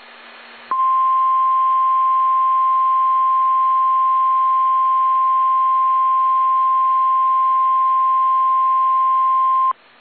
Emergency Alert System (SAME) tones (the 1050 Hz attention tone). It is heard from 0.7 to 9.8 s.
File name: eas / KSDZKDJL / KSDZKDJL